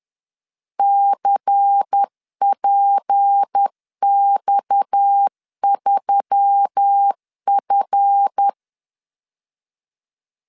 Morse code, heard as a keyed tone, about 10 wpm, 800 Hz. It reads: CPX3F
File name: LEZIONI / 7 GF9Z / 7 GF 10wpm 2514